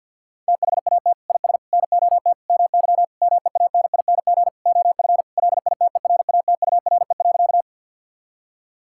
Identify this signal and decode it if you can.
Morse code, heard as a keyed tone, about 50 wpm, 700 Hz. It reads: THAT IS NOT MY MEANING OF LITERATURE.
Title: Sentences from Top 1000 Words - Review Part 6 - 50wpm